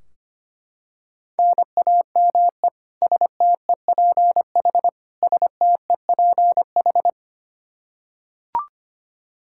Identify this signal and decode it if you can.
Morse code, heard as a keyed tone, about 25 wpm, 700 Hz. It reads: NAME STEPH STEPH
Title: Practice QSO - Exchange 1 - Beginning - 25wpm